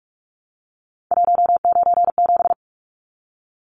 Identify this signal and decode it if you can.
Morse code, heard as a keyed tone, about 44 wpm, 700 Hz. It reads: 197
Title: Numbers - 3-Digit 45wpm